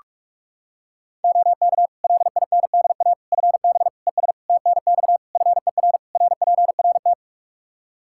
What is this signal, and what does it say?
Morse code, heard as a keyed tone, about 46 wpm, 700 Hz. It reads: OK LINDA FB ES TNX FER RPRT
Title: Practice QSO - Exchange 2 - Info - 45wpm